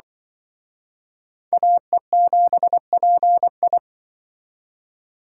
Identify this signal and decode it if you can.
Morse code, heard as a keyed tone, about 24 wpm, 700 Hz. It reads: AE7PI
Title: Speed Racing - USA 2x2 Call Signs 20wpm